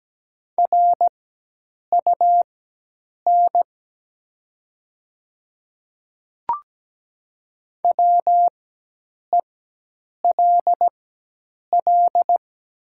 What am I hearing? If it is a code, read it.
Morse code, heard as a keyed tone, about 17 wpm, spaced as for 8 wpm, 700 Hz. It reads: RUN WELL
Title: 4x Character Spacing - Top 500 Words - 17wpm